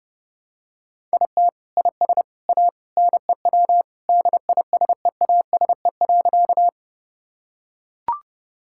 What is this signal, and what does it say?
Morse code, heard as a keyed tone, about 30 wpm, 700 Hz. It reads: IT IS A NEW DISEASE.
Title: Sentences from Top 1000 Words - Review Part 6 - 30wpm